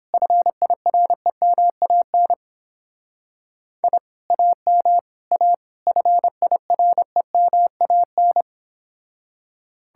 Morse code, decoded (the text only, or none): FIREMAN I AM A FIREMAN